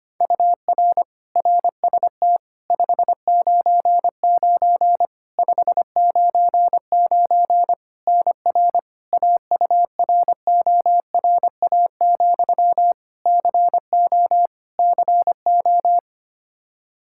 Morse code, heard as a keyed tone, about 25 wpm, 700 Hz. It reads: UR RST 599 599 NR AURORA, CO CO
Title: K1USN SST QSO Practice 25wpm